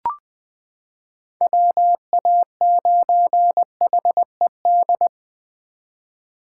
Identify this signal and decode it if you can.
Morse code, heard as a keyed tone, about 20 wpm, 700 Hz. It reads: WA9HED
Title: 2023 SOTA Chaser Call Signs - 20wpm